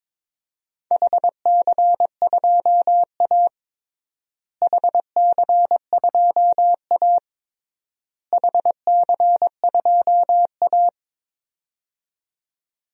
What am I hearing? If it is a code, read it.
Morse code, heard as a keyed tone, about 22 wpm, 700 Hz. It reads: HC2A HC2A HC2A
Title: DXCC-100 Call Signs - 4 Characters - Repeated 3x - 22wpm